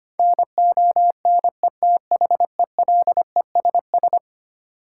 Morse code, heard as a keyed tone, about 25 wpm, 700 Hz. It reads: NONETHELESS
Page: morse-code-ninja-practice-25wpm